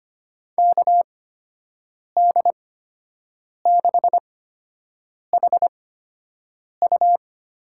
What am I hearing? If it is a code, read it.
Morse code, heard as a keyed tone, about 25 wpm, spaced as for 7 wpm, 700 Hz. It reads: KD6HU
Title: USA Call Signs - 6 Characters - 8x Character Spacing - 25wpm